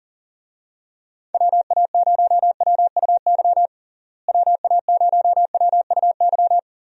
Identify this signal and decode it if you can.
Morse code, heard as a keyed tone, about 40 wpm, 700 Hz. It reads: WA0WUY WA0WUY
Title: USA Call Signs - 6 Characters - Repeated 3x - 40wpm